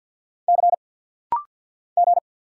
Morse code, heard as a keyed tone, about 50 wpm, 700 Hz. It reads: C G
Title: Single Letters - Rapid-Fire 50wpm